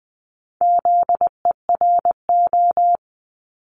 Morse code, heard as a keyed tone, about 20 wpm, 700 Hz. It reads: ZERO
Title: Multi-letter Phonemes 20wpm